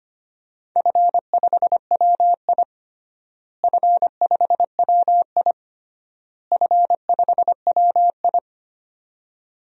Morse code, heard as a keyed tone, about 25 wpm, 700 Hz. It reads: F5WI F5WI F5WI
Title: DXCC-100 Call Signs - 4 Characters - Repeated 3x - 25wpm